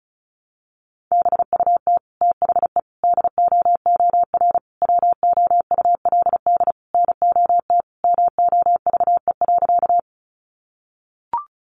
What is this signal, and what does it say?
Morse code, heard as a keyed tone, about 34 wpm, 700 Hz. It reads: BUT THE DOOR WOULD NOT MOVE.
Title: Sentences from Top 600 Words 35wpm